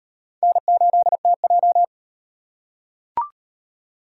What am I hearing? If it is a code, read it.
Morse code, heard as a keyed tone, about 36 wpm, 700 Hz. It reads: N8TJ